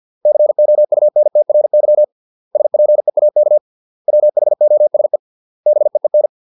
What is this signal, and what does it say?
Morse code, heard as a keyed tone, about 50 wpm, 600 Hz. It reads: COUNTRY SPEAK WHOSE BEEN